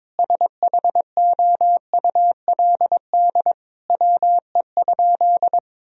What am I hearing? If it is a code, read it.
Morse code, heard as a keyed tone, about 22 wpm, 700 Hz. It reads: SHOULD WE?
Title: Sentences from Top 1000 Words - Review Part 3 - 22wpm